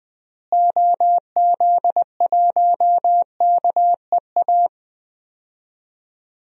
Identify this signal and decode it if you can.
Morse code, heard as a keyed tone, about 20 wpm, 700 Hz. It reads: OZ1KEA
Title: DXCC-100 Call Signs - 6 Characters - Repeated 2x - 20wpm